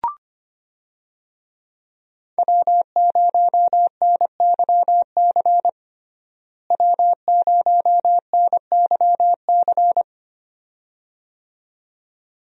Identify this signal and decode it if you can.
Morse code, heard as a keyed tone, about 25 wpm, 700 Hz. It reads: W0NYC W0NYC
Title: USA Call Signs - 5 Characters - Repeated 2x - 25wpm